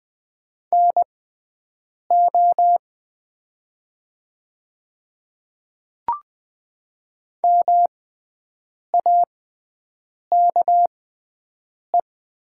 Morse code, heard as a keyed tone, about 20 wpm, spaced as for 7 wpm, 700 Hz. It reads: NO MAKE
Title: 6x Character Spacing - Top 100 Words - 20wpm